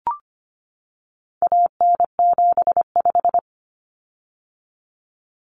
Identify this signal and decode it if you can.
Morse code, heard as a keyed tone, about 25 wpm, 700 Hz. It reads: AN75
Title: Common Cut Numbers - 4-Digit 25wpm